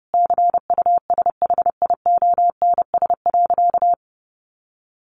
Morse code, heard as a keyed tone, about 30 wpm, 700 Hz. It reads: CUSHIONS.
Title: A Sound Of Thunder 30wpm